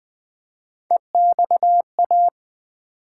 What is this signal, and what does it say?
Morse code, heard as a keyed tone, about 20 wpm, 700 Hz. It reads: EXA
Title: Build A Word - 1 Letter At A Time - 4-Letter Words - 20wpm